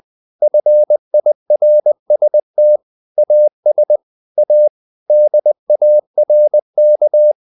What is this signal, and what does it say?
Morse code, heard as a keyed tone, about 20 wpm, 600 Hz. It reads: FIRST AS A DARK
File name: the old man and the sea part 2 20wpm